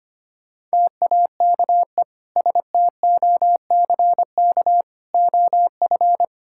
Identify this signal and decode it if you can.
Morse code, heard as a keyed tone, about 25 wpm, 700 Hz. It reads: TAKE STOCK OF
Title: Sentences from Top 1000 Words - Review Part 5 - 25wpm